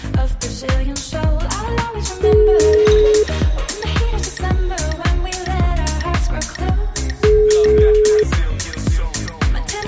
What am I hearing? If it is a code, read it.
Ringback tone (Europe, 425 Hz).